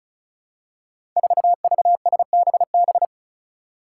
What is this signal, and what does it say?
Morse code, heard as a keyed tone, about 36 wpm, 700 Hz. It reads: 4VSBB
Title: Five Challenging Letter-Number Combinations 35wpm